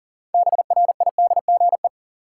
Morse code, heard as a keyed tone, about 40 wpm, 700 Hz. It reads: BRIDGE
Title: Sentences from Top 1200 Words - New Word Drill Alt - 40wpm